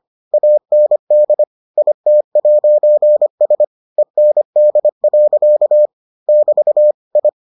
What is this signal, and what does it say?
Morse code, heard as a keyed tone, about 25 wpm, 600 Hz. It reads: AND IT'S END. <BT> I